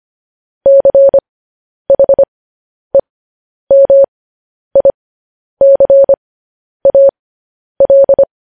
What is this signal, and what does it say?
Morse code, heard as a keyed tone, about 25 wpm, spaced as for 10 wpm, 550 Hz. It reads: CHEMICAL
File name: podcast / News headlines-1x-25x10